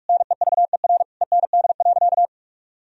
Morse code, heard as a keyed tone, about 41 wpm, 700 Hz. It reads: NEVER END.
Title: Sentences from Top 1000 Words - Review Part 3 - 45wpm